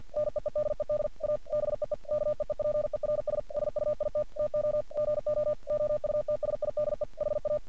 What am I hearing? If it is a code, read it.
Morse code, heard as a keyed tone, about 49 wpm, 600 Hz. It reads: NEEDED A BEE KEEPERS SUIT TO GO OUTSIDE HA
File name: archived / CRSnet